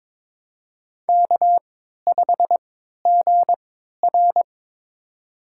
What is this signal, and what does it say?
Morse code, heard as a keyed tone, about 22 wpm, spaced as for 13 wpm, 700 Hz. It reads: K5GR
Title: USA Call Signs - 3x Character Spacing - 22wpm